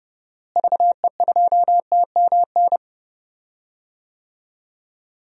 Morse code, heard as a keyed tone, about 31 wpm, 700 Hz. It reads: VE2TMN
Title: DXCC-100 Call Signs - 6 Characters - Repeated 2x - 30wpm